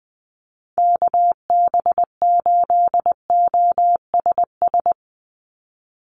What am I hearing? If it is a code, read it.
Morse code, heard as a keyed tone, about 19 wpm, 700 Hz. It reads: KB8OSS
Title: USA Call Signs 20wpm